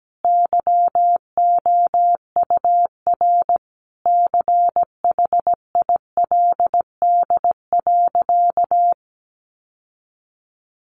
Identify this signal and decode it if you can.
Morse code, heard as a keyed tone, about 17 wpm, 700 Hz. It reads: YOUR CHILD.
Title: Sentences from Top 800 Words 17wpm